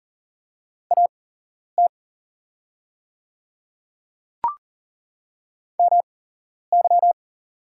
Morse code, heard as a keyed tone, about 40 wpm, spaced as for 11 wpm, 690 Hz. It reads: AT MY
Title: 8x Character Spacing - Top 100 Words - 40wpm